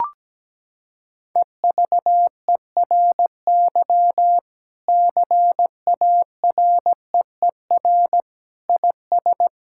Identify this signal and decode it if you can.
Morse code, heard as a keyed tone, about 17 wpm, 700 Hz. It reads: EVERY CAREER IS
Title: Sentences from Top 1000 Words - Review Part 6 - 17wpm